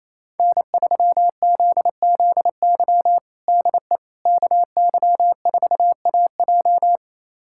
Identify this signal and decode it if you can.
Morse code, heard as a keyed tone, about 27 wpm, 700 Hz. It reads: N3ZZY DE KY4AJ
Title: USA Callsign DE Callsign - 5 Characters - 28wpm